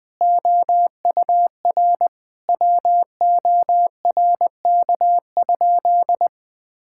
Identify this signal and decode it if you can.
Morse code, heard as a keyed tone, about 21 wpm, 700 Hz. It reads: OUR WORK?
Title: Sentences from Top 1000 Words - Review Part 6 - 20wpm